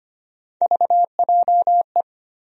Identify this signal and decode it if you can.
Morse code, heard as a keyed tone, about 25 wpm, 700 Hz. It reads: VJE
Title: USA Call Sign Postfixes - 3 Letters 25wpm